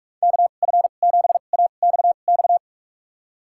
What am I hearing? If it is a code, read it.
Morse code, heard as a keyed tone, about 46 wpm, spaced as for 32 wpm, 700 Hz. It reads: KF7AXX
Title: USA Call Signs - 6 Characters - 2x Character Spacing - 45wpm